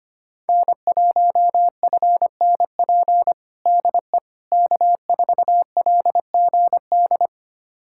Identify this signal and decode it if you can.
Morse code, heard as a keyed tone, about 25 wpm, 700 Hz. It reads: N1FNP DE K4LGD
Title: USA Callsign DE Callsign - 5 Characters - 25wpm